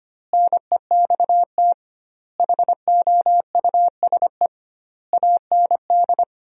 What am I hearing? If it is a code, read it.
Morse code, heard as a keyed tone, about 25 wpm, 700 Hz. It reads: NEXT HOUSE AND
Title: 2x Word-Spacing - Sentences from Top 400 Words - 25wpm